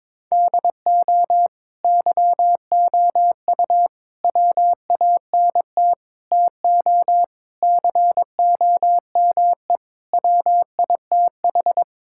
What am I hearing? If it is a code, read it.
Morse code, heard as a keyed tone, about 22 wpm, 700 Hz. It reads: DO YOU WANT TO COME WITH
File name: practice / 22wpm / Sentences with 3-Word Phrases - 22wpm